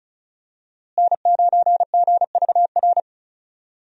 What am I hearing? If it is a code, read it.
Morse code, heard as a keyed tone, about 36 wpm, 700 Hz. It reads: N9GVR
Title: USA Call Signs - 5 Characters - Repeated 2x - 35wpm